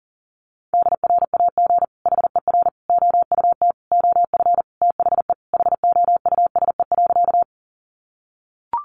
Morse code, heard as a keyed tone, about 40 wpm, 700 Hz. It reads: DRAG HER OUT OF THE HOUSE.
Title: Sentences from Top 1500 Words 40wpm